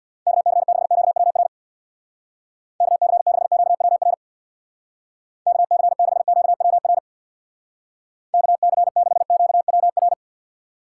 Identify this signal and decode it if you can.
Morse code, heard as a keyed tone, about 60 wpm, 700 Hz. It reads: KC6QWR KC6QWR KC6QWR KC6QWR
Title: Speed Racing - USA 2x3 Call Signs 50wpm